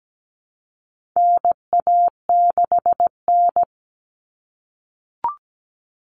Morse code, heard as a keyed tone, about 17 wpm, 700 Hz. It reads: NA6N
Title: Common Cut Numbers - 4-Digit 17wpm